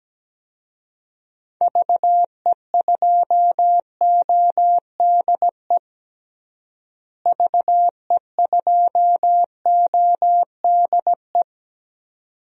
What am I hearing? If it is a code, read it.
Morse code, heard as a keyed tone, about 17 wpm, 700 Hz. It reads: VE2ODE VE2ODE
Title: DXCC-100 Call Signs - 6 Characters - Repeated 2x - 17wpm